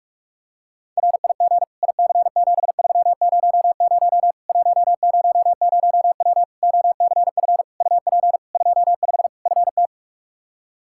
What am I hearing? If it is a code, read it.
Morse code, heard as a keyed tone, about 46 wpm, 700 Hz. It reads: RIG IC7300 100W OCF UP 25 FT
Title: Practice QSO - Exchange 2 - Info - 45wpm